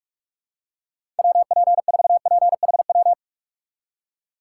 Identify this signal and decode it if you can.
Morse code, heard as a keyed tone, about 47 wpm, 700 Hz. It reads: WP4PHW
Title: USA Call Signs - 6 Characters - 4x Character Spacing - 45wpm